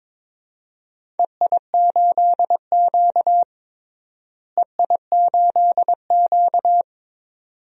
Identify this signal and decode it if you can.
Morse code, heard as a keyed tone, about 21 wpm, 700 Hz. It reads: EI8Q EI8Q
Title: DXCC-100 Call Signs - 4 Characters - Repeated 2x - 22wpm